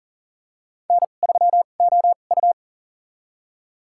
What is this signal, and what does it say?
Morse code, heard as a keyed tone, about 40 wpm, spaced as for 29 wpm, 700 Hz. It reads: N3OU
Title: USA Call Signs - 2x Character Spacing - 40wpm